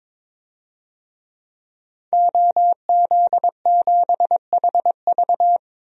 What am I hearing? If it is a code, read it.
Morse code, heard as a keyed tone, about 21 wpm, 700 Hz. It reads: OZ7HV
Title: DXCC-100 Call Signs - 5 Characters - 22wpm